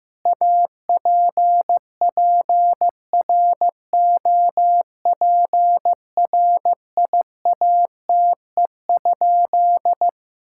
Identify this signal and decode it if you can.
Morse code, heard as a keyed tone, about 15 wpm, 700 Hz. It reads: APPROPRIATE?
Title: Sentences from Top 1000 Words - Review Part 5 - 15wpm